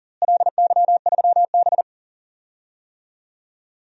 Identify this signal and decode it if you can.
Morse code, heard as a keyed tone, about 40 wpm, 700 Hz. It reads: LY3B
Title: DXCC-100 Call Signs - 4 Characters - Repeated 3x - 40wpm